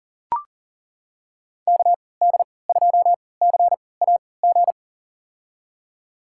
Morse code, heard as a keyed tone, about 40 wpm, spaced as for 23 wpm, 700 Hz. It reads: KD2CAG
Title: USA Call Signs - 6 Characters - 3x Character Spacing - 40wpm